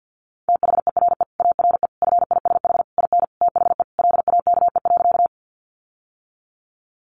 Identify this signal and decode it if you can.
Morse code, heard as a keyed tone, about 49 wpm, 700 Hz. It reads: THERE ARE FISH IN THE LAKE.
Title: Sentences from Top 1100 Words 50wpm